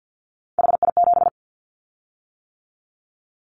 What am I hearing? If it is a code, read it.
Morse code, heard as a keyed tone, about 50 wpm, 700 Hz. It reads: HI7
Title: Callsign Prefixes for Top 100 DXCC - 50wpm